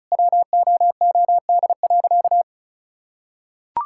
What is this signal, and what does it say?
Morse code, heard as a keyed tone, about 36 wpm, 700 Hz. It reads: WOOD.
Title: Sentences from Top 1600 Words 35wpm